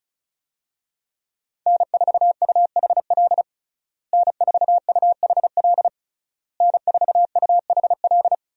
Morse code, heard as a keyed tone, about 36 wpm, 700 Hz. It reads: N4UHL N4UHL N4UHL
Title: USA Call Signs - 5 Characters - Repeated 3x - 35wpm